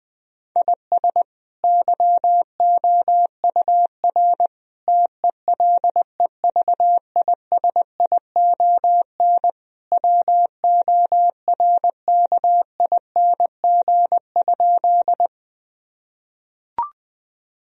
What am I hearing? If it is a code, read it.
Morse code, heard as a keyed tone, about 21 wpm, 700 Hz. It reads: IS YOUR TELEVISION WORKING?
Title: Sentences from Top 1000 Words - Review Part 4 - 20wpm